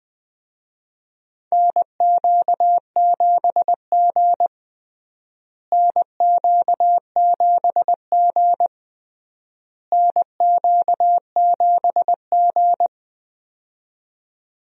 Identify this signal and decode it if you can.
Morse code, heard as a keyed tone, about 20 wpm, 700 Hz. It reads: NQ7G NQ7G NQ7G
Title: USA Call Signs - 4 Characters - Repeated 3x - 20wpm